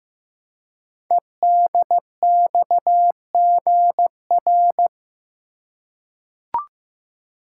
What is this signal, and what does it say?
Morse code, heard as a keyed tone, about 15 wpm, 700 Hz. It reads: EDXGR
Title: Code Groups - 5 Letters - 15wpm